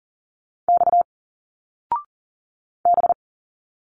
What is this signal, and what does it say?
Morse code, heard as a keyed tone, about 40 wpm, 700 Hz. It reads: X B
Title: Single Letters 40wpm